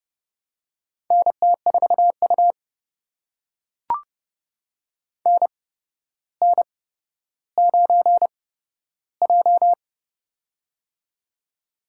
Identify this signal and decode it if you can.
Morse code, heard as a keyed tone, about 30 wpm, spaced as for 13 wpm, 700 Hz. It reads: NT4U NN9J
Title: USA Call Signs - 4 Characters - 8x Character Spacing - 30wpm